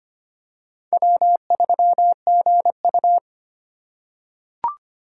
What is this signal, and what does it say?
Morse code, heard as a keyed tone, about 25 wpm, 700 Hz. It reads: W3GU
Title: USA Call Signs - 4 Characters - 2x Character Spacing - 25wpm